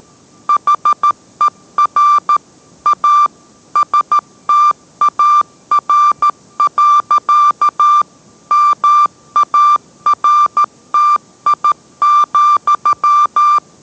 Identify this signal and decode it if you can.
Morse code, heard as a keyed tone, about 13 wpm, 1180 Hz. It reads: HER A STAR. MARTI,